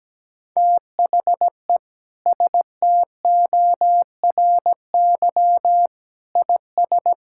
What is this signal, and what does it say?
Morse code, heard as a keyed tone, about 17 wpm, 700 Hz. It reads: THE STORY IS
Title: Sentences from Top 1000 Words - Review Part 5 - 17wpm